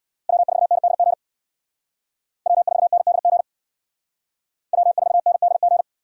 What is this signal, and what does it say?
Morse code, heard as a keyed tone, about 62 wpm, 700 Hz. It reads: W4NDG W4NDG W4NDG
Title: Speed Racing - USA 1x3 Call Signs 50wpm